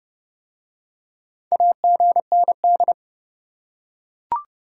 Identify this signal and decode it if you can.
Morse code, heard as a keyed tone, about 30 wpm, 700 Hz. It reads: AGND